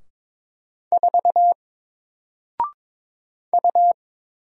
Morse code, heard as a keyed tone, about 22 wpm, 700 Hz. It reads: 4 U
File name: Single-Letter-Number-22wpm